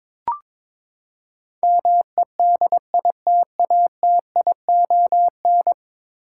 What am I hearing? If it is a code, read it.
Morse code, heard as a keyed tone, about 22 wpm, 700 Hz. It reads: MEDITATION
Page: morse-code-ninja-practice-22wpm